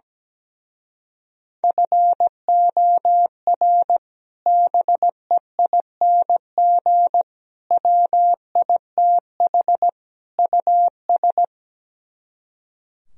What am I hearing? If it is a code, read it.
Morse code, heard as a keyed tone, about 17 wpm, 700 Hz. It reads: FOR BEING WITH US
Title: Speed-Racing - Sets of 4 Words 17wpm